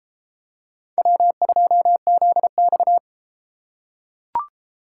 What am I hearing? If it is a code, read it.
Morse code, heard as a keyed tone, about 33 wpm, 700 Hz. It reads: W2ZX